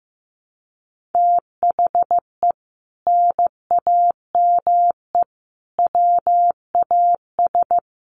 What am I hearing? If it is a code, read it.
Morse code, heard as a keyed tone, about 15 wpm, 700 Hz. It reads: THE NAME WAS
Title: Sentences from Top 1000 Words 15wpm